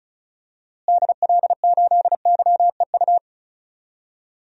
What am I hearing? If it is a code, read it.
Morse code, heard as a keyed tone, about 36 wpm, 700 Hz. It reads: DL8YEU
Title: DXCC-100 Call Signs - 6 Characters - Repeated 2x - 35wpm